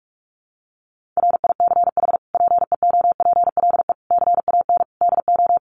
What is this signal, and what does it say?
Morse code, heard as a keyed tone, about 44 wpm, 700 Hz. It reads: RICH PEOPLE CAN DO
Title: Sentences from Top 1100 Words 45wpm